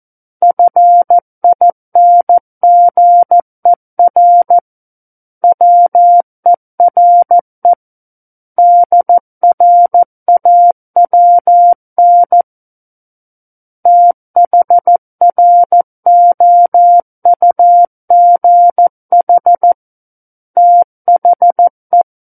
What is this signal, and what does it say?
Morse code, heard as a keyed tone, about 14 wpm, 700 Hz. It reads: FINGER WERE DRAWN THROUGH THE
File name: Book One / 05-Chapter 5 - 14 WPM